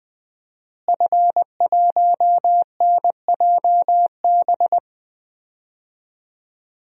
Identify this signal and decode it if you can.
Morse code, heard as a keyed tone, about 20 wpm, 700 Hz. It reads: F1NJB
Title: DXCC-100 Call Signs - 5 Characters - Repeated 2x - 20wpm